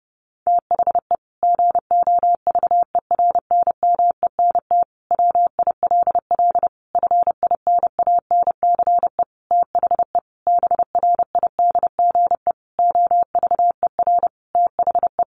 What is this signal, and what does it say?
Morse code, heard as a keyed tone, about 30 wpm, 700 Hz. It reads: THE GOVERNMENT WILL FINANCE THE BRIDGE OVER THE
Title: Sentences from Top 1500 Words 30wpm